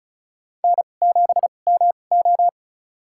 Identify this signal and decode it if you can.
Morse code, heard as a keyed tone, about 36 wpm, spaced as for 25 wpm, 700 Hz. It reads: N7MO
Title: USA Call Signs - 4 Characters - 2x Character Spacing - 35wpm